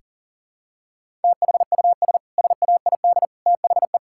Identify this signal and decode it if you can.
Morse code, heard as a keyed tone, about 40 wpm, 700 Hz. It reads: THUS SAID THE
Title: Sentences from Top 1000 Words - Review Part 5 - 40wpm